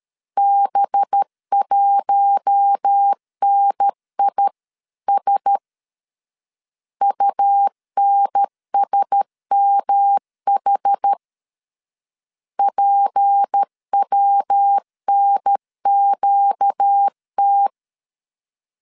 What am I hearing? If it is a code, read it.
Morse code, heard as a keyed tone, about 13 wpm, 800 Hz. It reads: B1NIS UNSMH PWNQT
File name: LEZIONI / 6 PKX0 / 6 P 12wpm 2033